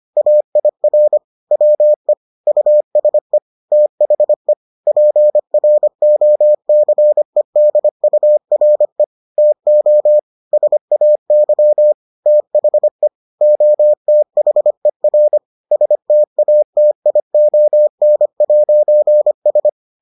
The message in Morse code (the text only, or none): AIR WE USE THE PROCEDURE TO SAY THE OTHER STATION'S